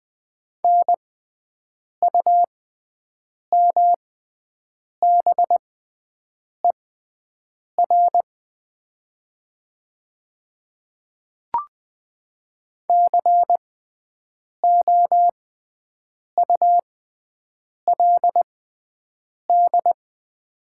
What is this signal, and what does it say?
Morse code, heard as a keyed tone, about 20 wpm, spaced as for 7 wpm, 700 Hz. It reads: NUMBER COULD
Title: 6x Character Spacing - Top 100 Words - 20wpm